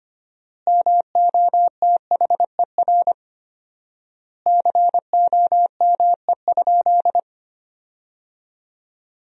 Morse code, heard as a keyed tone, about 25 wpm, 700 Hz. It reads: MOTHER COME?
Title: 4x Word-Spacing - Sentences from Top 200 Words - 25wpm